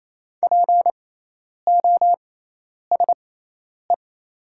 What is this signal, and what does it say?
Morse code, heard as a keyed tone, about 28 wpm, spaced as for 10 wpm, 700 Hz. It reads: POSE
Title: 6x Character Spacing - Top 500 Words - 28wpm